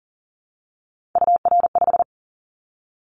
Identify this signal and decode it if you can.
Morse code, heard as a keyed tone, about 38 wpm, 700 Hz. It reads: UR5
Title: Callsign Prefixes for Top 25 DXCC - 40wpm